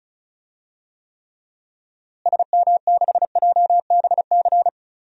Morse code, heard as a keyed tone, about 34 wpm, 700 Hz. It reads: SM6JBC
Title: DXCC-100 Call Signs - 6 Characters - Repeated 2x - 35wpm